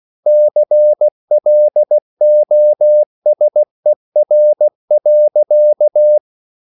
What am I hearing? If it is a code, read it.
Morse code, heard as a keyed tone, about 16 wpm, 600 Hz. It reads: CLOSER.
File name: CW/Practice